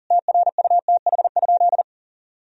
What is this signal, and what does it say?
Morse code, heard as a keyed tone, about 40 wpm, 700 Hz. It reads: TRUTH?
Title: Sentences from Top 1000 Words - Review Part 4 - 40wpm